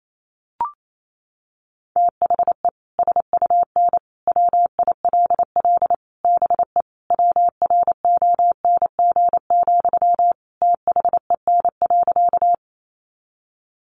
Morse code, heard as a keyed tone, about 28 wpm, 700 Hz. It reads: THE SUN WILL BE WRONG, THEN.
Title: Sentences from Top 900 Words 28wpm